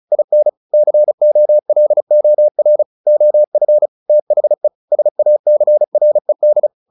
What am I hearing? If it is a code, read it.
Morse code, heard as a keyed tone, about 34 wpm, 600 Hz. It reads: IN COLOR OF THE SACRED
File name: the old man and the sea / the old man and the sea 1 35wpm